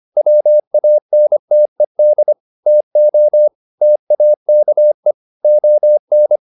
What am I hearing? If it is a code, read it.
Morse code, heard as a keyed tone, about 25 wpm, 600 Hz. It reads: WANTED TO TAKE ON